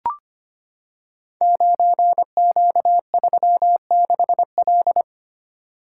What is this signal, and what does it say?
Morse code, heard as a keyed tone, about 25 wpm, 700 Hz. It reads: 9Q36L
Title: Code Groups - 5 Letter:Numbers - 25wpm